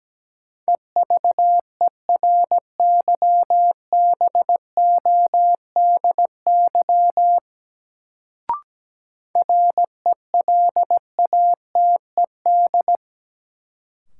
Morse code, heard as a keyed tone, about 17 wpm, 700 Hz. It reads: EVERYBODY RELATED
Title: Sentences from Top 1400 Words - New Word Drill Alt - 17wpm